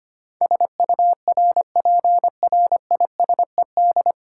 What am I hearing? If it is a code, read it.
Morse code, heard as a keyed tone, about 25 wpm, 700 Hz. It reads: SURPRISED
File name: practice / 25wpm / 9 Letter Words - First 3 Letters Given - 25wpm